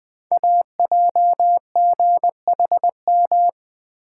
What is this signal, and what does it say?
Morse code, heard as a keyed tone, about 20 wpm, 700 Hz. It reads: AJGHM
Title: Code Groups - 5 Letters - 20wpm